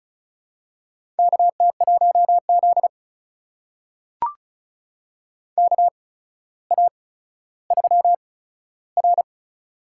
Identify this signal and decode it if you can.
Morse code, heard as a keyed tone, about 36 wpm, spaced as for 15 wpm, 700 Hz. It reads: KT1Z KA3R
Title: USA Call Signs - 4 Characters - 8x Character Spacing - 35wpm